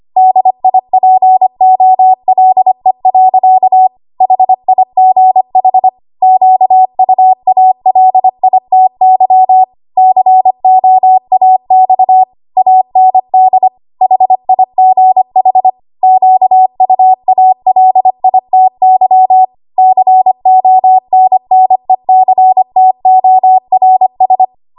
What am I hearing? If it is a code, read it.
Morse code, heard as a keyed tone, about 25 wpm, 750 Hz. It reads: DIPOLE. HIGH QUALITY COAX AND HIGH QUALITY CONNECTORS
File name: Archive / 25 WPM / 25WPM